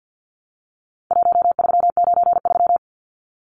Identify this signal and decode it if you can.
Morse code, heard as a keyed tone, about 50 wpm, 700 Hz. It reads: 1393
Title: Numbers - 4-Digit 50wpm